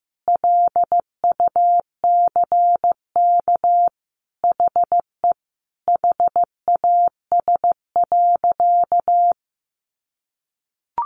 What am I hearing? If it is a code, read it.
Morse code, heard as a keyed tone, about 15 wpm, 700 Hz. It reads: LUCK HE HAS.
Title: Sentences from Top 1500 Words 15wpm